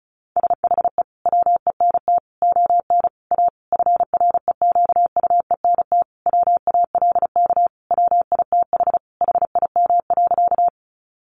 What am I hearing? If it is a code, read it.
Morse code, heard as a keyed tone, about 34 wpm, 700 Hz. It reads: SHE WENT ON A FREQUENT WALK WITH HIM.
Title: Sentences from Top 1400 Words 35wpm